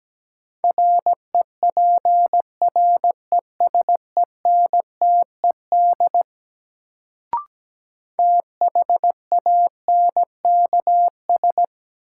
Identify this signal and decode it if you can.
Morse code, heard as a keyed tone, about 17 wpm, 700 Hz. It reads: REPRESENTED THANKS
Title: Sets of 5 Words - New Word Drill - 17wpm